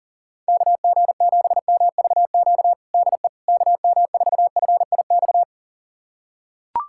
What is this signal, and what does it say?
Morse code, heard as a keyed tone, about 40 wpm, 700 Hz. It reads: KG7MVQ DE KM4FIX
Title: USA Callsign DE Callsign 40wpm